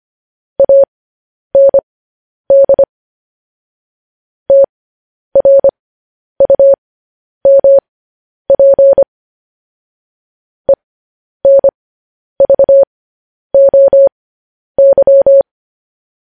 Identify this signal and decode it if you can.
Morse code, heard as a keyed tone, about 25 wpm, spaced as for 10 wpm, 550 Hz. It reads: AND TRUMP ENVOY
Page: News Headlines in Morse Code at 10 WPM